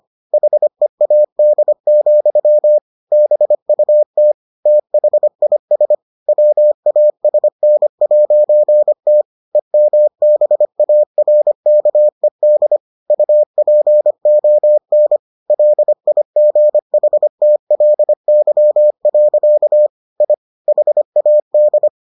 Morse code, heard as a keyed tone, about 25 wpm, 600 Hz. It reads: HEAD, BUT THIS WASN'T EMBARKED UPON LIGHTLY. I HAD